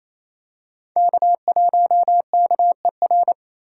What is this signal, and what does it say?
Morse code, heard as a keyed tone, about 28 wpm, 700 Hz. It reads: K1KER
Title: USA Call Signs - 5 Characters - 2x Character Spacing - 28wpm